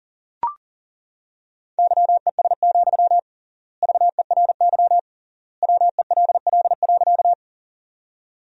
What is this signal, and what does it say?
Morse code, heard as a keyed tone, about 40 wpm, 700 Hz. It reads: YES, VERY WELL.
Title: 3x Word-Spacing - Sentences from Top 500 Words - 40wpm